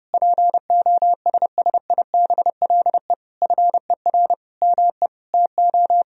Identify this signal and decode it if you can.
Morse code, heard as a keyed tone, about 30 wpm, 700 Hz. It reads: POSSIBLE FER ME TO
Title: Sentences from Top 500 Words and Abbreviations 30wpm